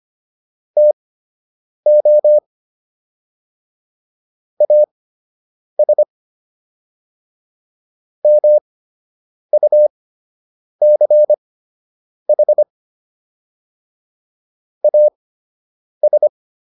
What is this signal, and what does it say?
Morse code, heard as a keyed tone, about 25 wpm, spaced as for 8 wpm, 600 Hz. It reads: TO AS MUCH AS